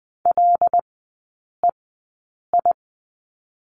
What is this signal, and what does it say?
Morse code, heard as a keyed tone, about 20 wpm, spaced as for 8 wpm, 700 Hz. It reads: LEI
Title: 074 Lesson 13e - Letter L - Three Characters 20wpm